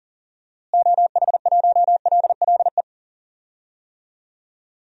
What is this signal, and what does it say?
Morse code, heard as a keyed tone, about 40 wpm, 700 Hz. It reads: OH1LLE
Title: DXCC-100 Call Signs - 6 Characters - Repeated 2x - 40wpm